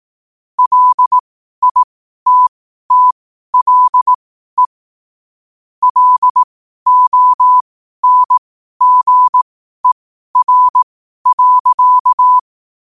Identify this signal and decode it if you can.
Morse code, heard as a keyed tone, about 18 wpm, spaced as for 12 wpm, 1000 Hz. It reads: LITTLE LONGER.
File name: qotdfiles / 12wpm